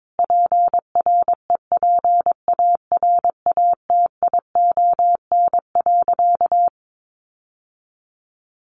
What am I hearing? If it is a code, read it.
Morse code, heard as a keyed tone, about 22 wpm, 700 Hz. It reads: PREPARATION.
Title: Sentences from Top 1400 Words 22wpm